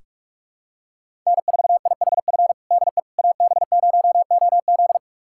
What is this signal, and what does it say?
Morse code, heard as a keyed tone, about 48 wpm, 700 Hz. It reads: N4IHF DE AB0OZ
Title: USA Callsign DE Callsign - 5 Characters - 45wpm